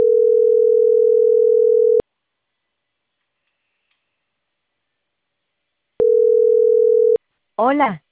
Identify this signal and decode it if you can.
Ringback tone (North America), 440 + 480 Hz, with speech.
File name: CENTRAL CAEA / out agcaea01